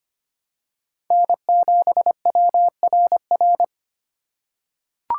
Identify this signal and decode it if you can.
Morse code, heard as a keyed tone, about 25 wpm, 700 Hz. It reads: N7WRR
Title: USA Call Signs - 5 Characters - 2x Character Spacing - 25wpm